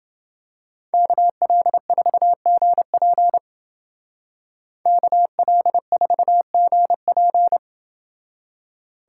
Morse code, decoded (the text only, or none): KL4GP KL4GP